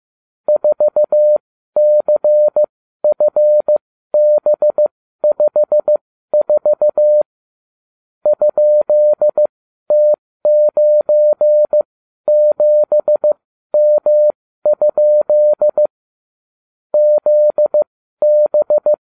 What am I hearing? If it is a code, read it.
Morse code, heard as a keyed tone, about 15 wpm, spaced as for 12 wpm, 600 Hz. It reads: 4CFB54 ?T97M? ZB